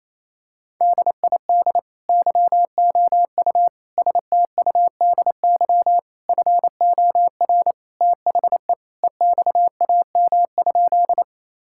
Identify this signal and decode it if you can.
Morse code, heard as a keyed tone, about 28 wpm, 700 Hz. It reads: DID YOU STUDY FOR THE EXAM?
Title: Sentences from Top 1000 Words - Review Part 5 - 28wpm